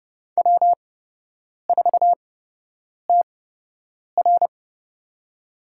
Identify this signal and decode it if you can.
Morse code, heard as a keyed tone, about 30 wpm, spaced as for 8 wpm, 700 Hz. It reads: W4TR
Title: USA Call Signs - 8x Character Spacing - 30wpm